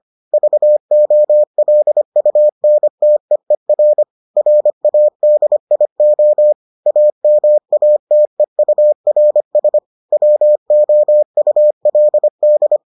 Morse code, heard as a keyed tone, about 25 wpm, 600 Hz. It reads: VOLUNTEER RADIO AMATEURS WOULD